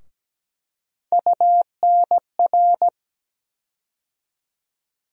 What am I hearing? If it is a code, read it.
Morse code, heard as a keyed tone, about 17 wpm, 700 Hz. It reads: UNR
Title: USA Call Sign Postfixes - 3 Letters 17wpm